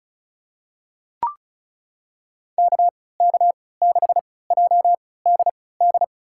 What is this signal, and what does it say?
Morse code, heard as a keyed tone, about 36 wpm, spaced as for 20 wpm, 700 Hz. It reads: KK6JDD
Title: USA Call Signs - 3x Character Spacing - 35wpm